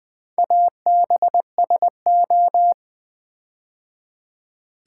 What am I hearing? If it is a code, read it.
Morse code, heard as a keyed tone, about 20 wpm, 700 Hz. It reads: ABSO
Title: Build A Word - 4 Letters At A Time - 10-Letter Words - 20wpm